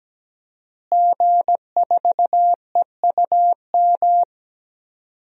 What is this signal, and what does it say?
Morse code, heard as a keyed tone, about 17 wpm, 700 Hz. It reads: G4EUM
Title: DXCC-100 Call Signs - 5 Characters - Repeated 3x - 17wpm